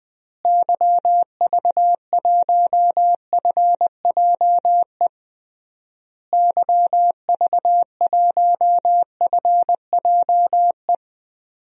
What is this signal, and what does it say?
Morse code, heard as a keyed tone, about 20 wpm, 700 Hz. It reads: YV1FJE YV1FJE
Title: DXCC-100 Call Signs - 6 Characters - Repeated 3x - 20wpm